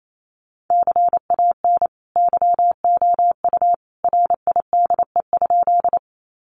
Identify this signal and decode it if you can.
Morse code, heard as a keyed tone, about 28 wpm, 700 Hz. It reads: CAN YOU RIDE?
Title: Sentences from Top 1200 Words 28wpm